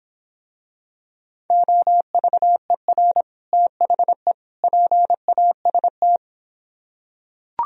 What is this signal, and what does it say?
Morse code, heard as a keyed tone, about 25 wpm, 700 Hz. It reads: OVER THE PAST